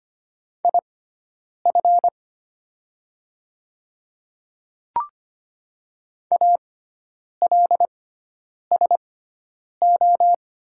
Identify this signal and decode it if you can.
Morse code, heard as a keyed tone, about 25 wpm, spaced as for 9 wpm, 700 Hz. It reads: IF ALSO
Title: 6x Character Spacing - Top 200 Words - 25wpm